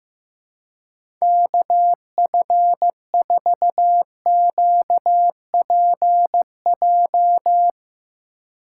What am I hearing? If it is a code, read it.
Morse code, heard as a keyed tone, about 15 wpm, 700 Hz. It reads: KF4QPJ
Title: USA Call Signs - 2x Character Spacing - 15wpm